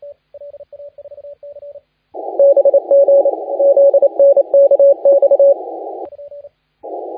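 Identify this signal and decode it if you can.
Morse code, heard as a keyed tone, about 28 wpm, 560 Hz, with radio noise. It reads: BG ZNKX
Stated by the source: Morse contest audio recorded by LA4C during the SAC CW 2018 contest.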